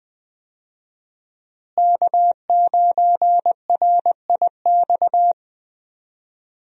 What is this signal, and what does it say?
Morse code, heard as a keyed tone, about 20 wpm, 700 Hz. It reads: K9RIX
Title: USA Call Signs - 5 Characters - Repeated 2x - 20wpm